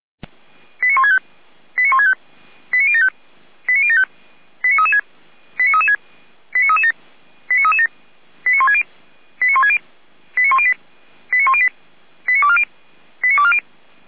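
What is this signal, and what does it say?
Five-tone selective call (ZVEI), heard as quick runs of five tones, tones of about 70 ms. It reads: 89165 89165 89075 89075 89336 89336 89337 89337 89160 89160 89108 89108 89240 89240
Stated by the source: ZVEI five-tone selective call received on SDR, from the feuerwehr-orlamuende.org (fire brigade Orlamünde) audio server.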